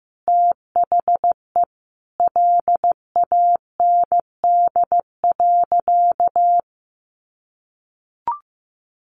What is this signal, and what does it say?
Morse code, heard as a keyed tone, about 14 wpm, 700 Hz. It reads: THE LAND.
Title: Sentences from Top 1200 Words 15wpm